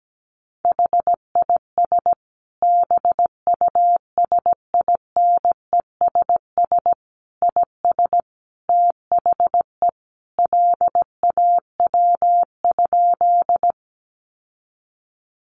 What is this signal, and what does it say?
Morse code, heard as a keyed tone, about 17 wpm, 700 Hz. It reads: HIS BUSINESS IS THE LAW?
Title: Sentences from Top 600 Words 17wpm